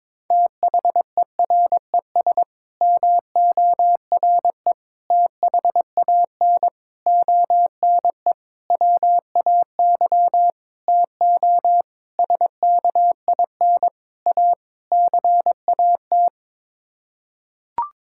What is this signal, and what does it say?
Morse code, heard as a keyed tone, about 22 wpm, 700 Hz. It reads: THERES MORE THAN ONE WAY TO SKIN A CAT
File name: Sets of 9 Words - 22wpm